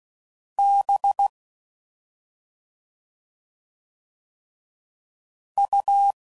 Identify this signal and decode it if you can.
Morse code, heard as a keyed tone, about 16 wpm, 800 Hz. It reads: B U
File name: Suffixes / Special ING's / Sngl ING's